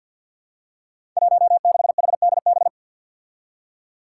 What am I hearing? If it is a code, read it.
Morse code, heard as a keyed tone, about 49 wpm, 700 Hz. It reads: 16HDB
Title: Five Challenging Letter-Number Combinations 50wpm